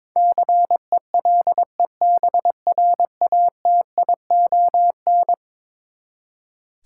Morse code, heard as a keyed tone, about 22 wpm, 700 Hz. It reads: CELEBRATION
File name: practice / 22wpm / Eleven Letter Words - 22wpm